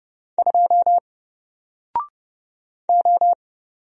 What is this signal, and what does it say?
Morse code, heard as a keyed tone, about 31 wpm, 700 Hz. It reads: 2 O
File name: Single-Letter-Number-30wpm